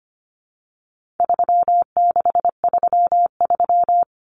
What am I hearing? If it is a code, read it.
Morse code, heard as a keyed tone, about 25 wpm, 700 Hz. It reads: 3633
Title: Numbers - 4-Digit 25wpm